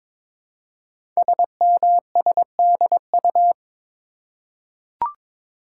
Morse code, heard as a keyed tone, about 22 wpm, 700 Hz. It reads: SMSDU
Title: Code Groups - 5 Letters - 22wpm